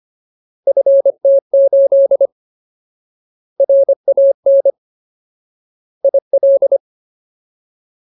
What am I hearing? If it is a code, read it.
Morse code, heard as a keyed tone, about 25 wpm, 550 Hz. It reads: FT8 RAN IL
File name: Ultimate Word List / 25WPM Ultimate